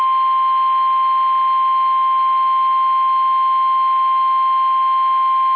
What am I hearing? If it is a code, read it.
Emergency Alert System (SAME) tones (the 1050 Hz attention tone). It is heard from 0.4 to 5.0 s.
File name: eas / KUSO / KUSO